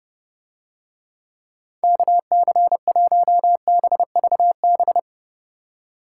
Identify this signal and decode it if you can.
Morse code, heard as a keyed tone, about 29 wpm, 700 Hz. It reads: KC1BVB